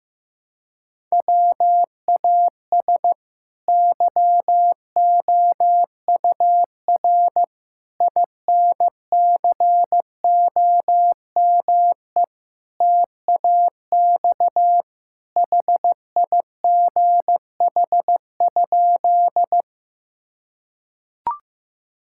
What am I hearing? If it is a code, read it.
Morse code, heard as a keyed tone, about 15 wpm, 700 Hz. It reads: WAS YOUR INCOME TAX HIGH?
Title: Sentences from Top 1000 Words - Review Part 5 - 15wpm